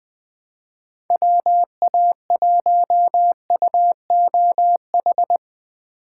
Morse code, heard as a keyed tone, about 20 wpm, 700 Hz. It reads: WA1UOH
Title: Speed Racing - USA 2x3 Call Signs 17wpm